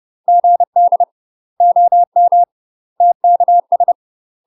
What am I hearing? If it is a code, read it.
Morse code, heard as a keyed tone, about 30 wpm, 700 Hz. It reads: GD OM TKS
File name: UK MASTER QSOs TRAINING / 30WPM / UK MASTER QSOs 0027